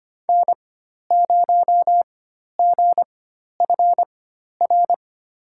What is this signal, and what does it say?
Morse code, heard as a keyed tone, about 25 wpm, spaced as for 12 wpm, 700 Hz. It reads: N0GFR
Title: USA Call Signs - 5 Characters - 4x Character Spacing - 25wpm